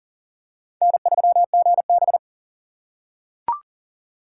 Morse code, heard as a keyed tone, about 40 wpm, 700 Hz. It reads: N3GB
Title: USA Call Signs - 6x Character Spacing - 40wpm